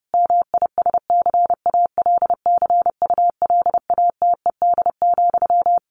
Morse code, heard as a keyed tone, about 28 wpm, 700 Hz. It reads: MISCALCULATED,
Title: A Sound Of Thunder 30wpm